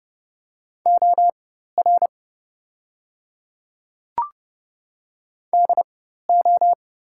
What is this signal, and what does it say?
Morse code, heard as a keyed tone, about 30 wpm, spaced as for 14 wpm, 700 Hz. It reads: OR DO
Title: 4x Character Spacing - Top 200 Words - 30wpm